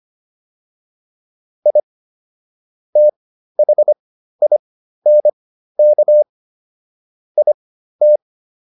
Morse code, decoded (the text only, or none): I THINK IT